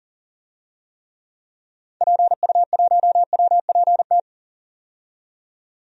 Morse code, heard as a keyed tone, about 40 wpm, 700 Hz. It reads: PU1WPT
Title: DXCC-100 Call Signs - 6 Characters - 40wpm